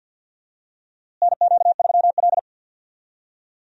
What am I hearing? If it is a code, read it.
Morse code, heard as a keyed tone, about 53 wpm, 700 Hz. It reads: NQ3L
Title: USA Call Signs - 4 Characters - Repeated 2x - 50wpm